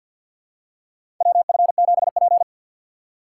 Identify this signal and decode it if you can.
Morse code, heard as a keyed tone, about 50 wpm, 700 Hz. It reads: WF7P
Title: USA Call Signs - 4 Characters - 2x Character Spacing - 50wpm